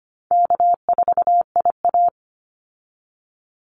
Morse code, heard as a keyed tone, about 25 wpm, 700 Hz. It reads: K4IA
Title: Easy Call Signs Repeated 2x - 25wpm